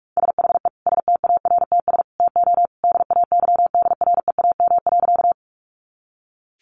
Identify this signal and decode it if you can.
Morse code, heard as a keyed tone, about 46 wpm, 700 Hz. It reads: SHE STARTS TO DAYDREAM.
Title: Easy Phrases 45wpm